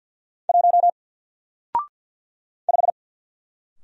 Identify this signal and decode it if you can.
Morse code, heard as a keyed tone, about 48 wpm, 700 Hz. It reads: 1 5
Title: Single Letter-Number Alt 50wpm